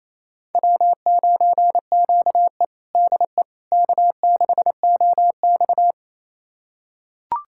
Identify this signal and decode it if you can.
Morse code, heard as a keyed tone, about 28 wpm, 700 Hz. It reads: W9QE DE K6OX
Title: USA Callsign DE Callsign - 4 Characters - 28wpm